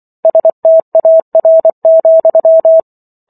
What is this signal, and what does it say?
Morse code, heard as a keyed tone, about 24 wpm, 650 Hz. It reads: STAR,